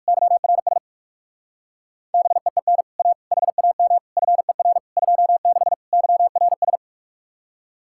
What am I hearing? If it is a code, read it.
Morse code, heard as a keyed tone, about 58 wpm, 700 Hz. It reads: YRS BEEN A HAM FER 26 YRS